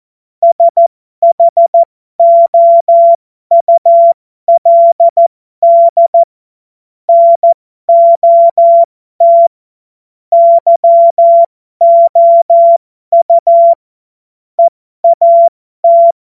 Morse code, heard as a keyed tone, about 14 wpm, spaced as for 12 wpm, 680 Hz. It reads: SHOULD NOT YOU EAT
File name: Morsekurs / Witze